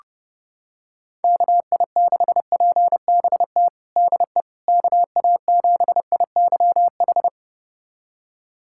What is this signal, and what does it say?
Morse code, heard as a keyed tone, about 30 wpm, 700 Hz. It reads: KI6PBT DE KA7IYH
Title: USA Callsign DE Callsign - 6 Characters - 30wpm